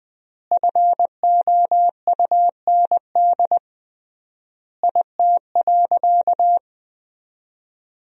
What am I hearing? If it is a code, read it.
Morse code, heard as a keyed tone, about 20 wpm, 700 Hz. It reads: FOUND IT.
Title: 3x Word-Spacing - Sentences from Top 300 Words - 20wpm